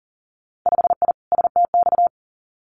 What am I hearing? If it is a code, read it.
Morse code, heard as a keyed tone, about 41 wpm, 700 Hz. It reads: 5I STX
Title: Winter Field Day Exchanges Encore 40wpm